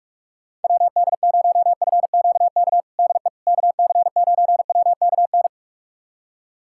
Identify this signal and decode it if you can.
Morse code, heard as a keyed tone, about 44 wpm, 700 Hz. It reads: WD0FQK DE KC9WKN
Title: USA Callsign DE Callsign 45wpm